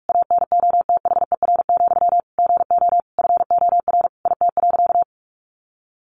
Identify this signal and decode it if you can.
Morse code, heard as a keyed tone, about 44 wpm, 700 Hz. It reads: ANOTHER, GO FOR IT.
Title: Sentences from Top 1100 Words 45wpm